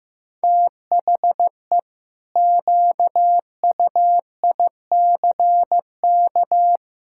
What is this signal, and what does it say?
Morse code, heard as a keyed tone, about 15 wpm, 700 Hz. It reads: THE QUICK
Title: Pangram Challenge v2 - 15wpm